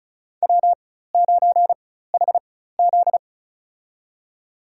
Morse code, heard as a keyed tone, about 35 wpm, spaced as for 16 wpm, 700 Hz. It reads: W9HZ
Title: USA Call Signs - 4 Characters - 4x Character Spacing - 35wpm